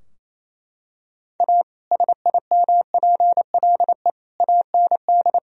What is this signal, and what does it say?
Morse code, heard as a keyed tone, about 27 wpm, 700 Hz. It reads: A SIMPLE AND
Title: Sentences from Top 1000 Words - Review Part 4 - 28wpm